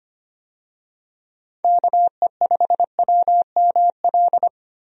Morse code, heard as a keyed tone, about 25 wpm, 700 Hz. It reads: KE5WML
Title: USA Call Signs - 6 Characters - Repeated 2x - 25wpm